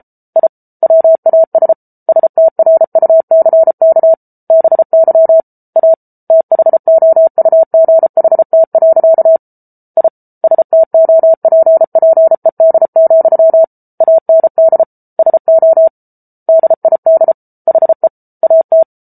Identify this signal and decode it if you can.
Morse code, heard as a keyed tone, about 34 wpm, 650 Hz. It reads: I WAS STRUCK BY A THOUGHT. I STOPPED, AND SO DID HE AT